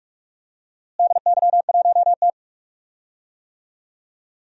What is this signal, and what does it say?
Morse code, heard as a keyed tone, about 46 wpm, 700 Hz. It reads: DY1T
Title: DXCC-100 Call Signs - 4 Characters - Repeated 3x - 45wpm